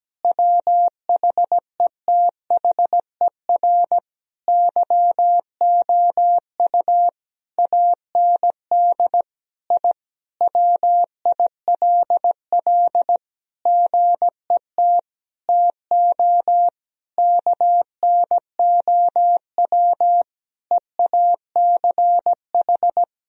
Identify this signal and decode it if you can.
Morse code, heard as a keyed tone, about 17 wpm, 700 Hz. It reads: WHETHER YOU AND I WILL GET TO KNOW EACH
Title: Sentences from Top 1000 Words - Review Part 3 - 17wpm